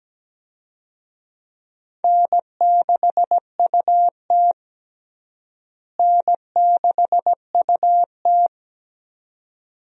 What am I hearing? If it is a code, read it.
Morse code, heard as a keyed tone, about 17 wpm, 700 Hz. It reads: N6UT N6UT
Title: USA Call Signs - 4 Characters - Repeated 2x - 17wpm